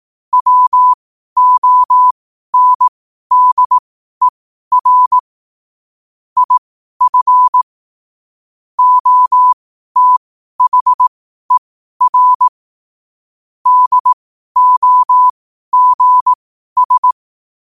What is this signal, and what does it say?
Morse code, heard as a keyed tone, about 18 wpm, spaced as for 12 wpm, 1000 Hz. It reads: WONDER IF OTHER DOGS
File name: qotdfiles / 12wpm